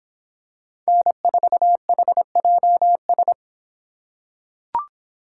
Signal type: Morse code

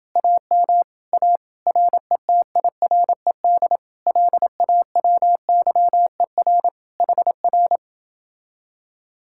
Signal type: Morse code

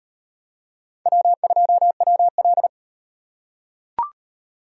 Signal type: Morse code

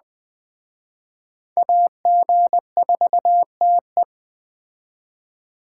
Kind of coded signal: Morse code

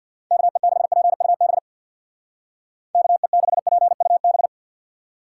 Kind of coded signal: Morse code

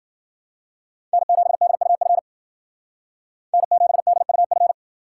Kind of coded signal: Morse code

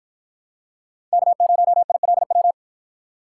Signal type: Morse code

